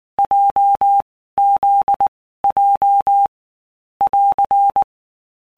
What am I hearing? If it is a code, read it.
Morse code, heard as a keyed tone, about 19 wpm, 800 Hz. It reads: J Z J <AR>
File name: All Groups 1-minute Blocks / Groups@18 / JZ17